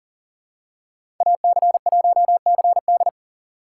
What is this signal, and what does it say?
Morse code, heard as a keyed tone, about 38 wpm, 700 Hz. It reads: AC1CD
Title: USA Call Signs - 5 Characters - 2x Character Spacing - 40wpm